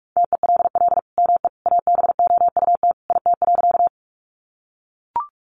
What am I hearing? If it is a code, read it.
Morse code, heard as a keyed tone, about 44 wpm, 700 Hz. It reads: TELL ME ABOUT IT.
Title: Sentences from Top 200 Words 45wpm